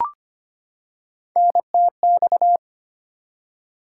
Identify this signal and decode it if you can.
Morse code, heard as a keyed tone, about 25 wpm, 700 Hz. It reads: NTX
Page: morse-code-ninja-practice-25wpm